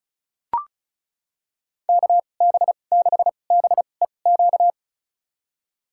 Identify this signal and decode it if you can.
Morse code, heard as a keyed tone, about 34 wpm, spaced as for 25 wpm, 700 Hz. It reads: KB6BEQ
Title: USA Call Signs - 2x Character Spacing - 35wpm